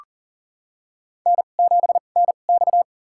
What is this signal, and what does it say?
Morse code, heard as a keyed tone, about 40 wpm, spaced as for 29 wpm, 700 Hz. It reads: N7NX
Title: USA Call Signs - 4 Characters - 2x Character Spacing - 40wpm